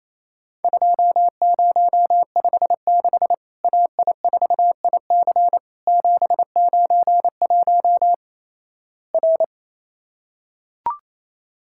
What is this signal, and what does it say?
Morse code, heard as a keyed tone, about 28 wpm, 700 Hz. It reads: 2056 AI4IC 791 R
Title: Worked All Europe DX Contest - QTC Exchange - 28wpm